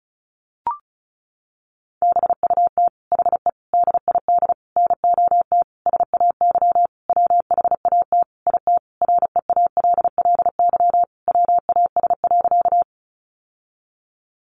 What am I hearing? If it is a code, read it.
Morse code, heard as a keyed tone, about 34 wpm, 700 Hz. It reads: BUT HE DID NOT SAY WHAT IT REALLY WAS.
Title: Sentences from Top 600 Words 35wpm